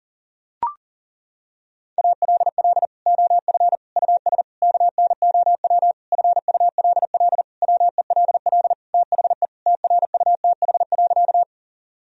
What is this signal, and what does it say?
Morse code, heard as a keyed tone, about 40 wpm, 700 Hz. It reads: ALL OF US KNOW FULL WELL THE TRUTH.
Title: Sentences from Top 1000 Words - Review Part 3 - 40wpm